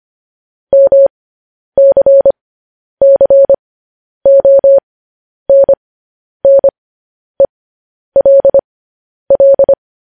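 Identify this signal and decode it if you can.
Morse code, heard as a keyed tone, about 25 wpm, spaced as for 10 wpm, 550 Hz. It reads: MCCONNELL
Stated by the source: Morse code news headlines sent as a reading transcription, preceded by vvv.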